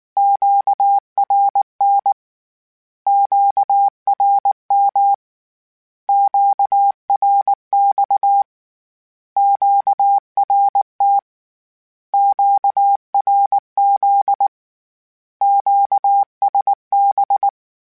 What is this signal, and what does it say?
Morse code, heard as a keyed tone, about 19 wpm, 800 Hz. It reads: QRN QRM QRX QRT QRZ QSB
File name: Q Calls / Q Calls-mp3s / Q Calls Once @18